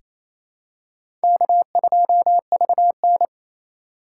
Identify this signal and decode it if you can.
Morse code, heard as a keyed tone, about 28 wpm, 700 Hz. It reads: K2VN
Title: USA Call Signs - 4 Characters - 2x Character Spacing - 28wpm